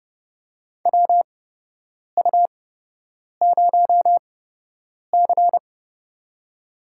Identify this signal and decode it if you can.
Morse code, heard as a keyed tone, about 29 wpm, spaced as for 8 wpm, 700 Hz. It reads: WU0C
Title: USA Call Signs - 4 Characters - 8x Character Spacing - 30wpm